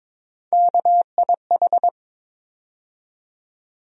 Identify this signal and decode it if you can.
Morse code, heard as a keyed tone, about 22 wpm, 700 Hz. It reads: KIH
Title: USA Call Sign Postfixes - 3 Letters 22wpm